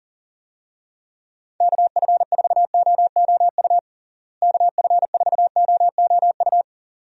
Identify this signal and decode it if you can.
Morse code, heard as a keyed tone, about 38 wpm, 700 Hz. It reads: KF4OOU KF4OOU
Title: USA Call Signs - 6 Characters - Repeated 2x - 40wpm